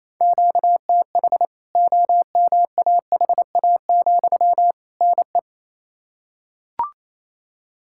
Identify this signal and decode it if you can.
Morse code, heard as a keyed tone, about 28 wpm, 700 Hz. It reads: QTH OMAHA, NE
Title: K1USN SST QSO Practice 28wpm